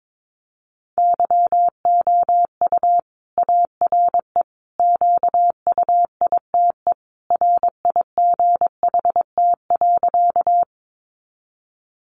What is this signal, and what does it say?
Morse code, heard as a keyed tone, about 22 wpm, 700 Hz. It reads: YOU ARE QUITE RIGHT.
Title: Sentences from Top 700 Words 22wpm